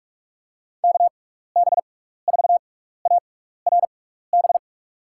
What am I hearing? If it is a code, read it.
Morse code, heard as a keyed tone, about 44 wpm, spaced as for 15 wpm, 700 Hz. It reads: KB4ARB
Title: USA Call Signs - 6 Characters - 6x Character Spacing - 45wpm